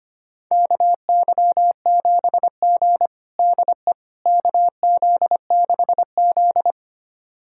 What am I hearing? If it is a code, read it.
Morse code, heard as a keyed tone, about 25 wpm, 700 Hz. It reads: KY7G DE KZ6Z
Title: USA Callsign DE Callsign - 4 Characters - 25wpm